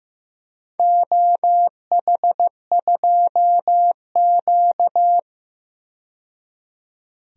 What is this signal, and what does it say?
Morse code, heard as a keyed tone, about 15 wpm, 700 Hz. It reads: OH2Q
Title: DXCC-100 Call Signs - 4 Characters - 15wpm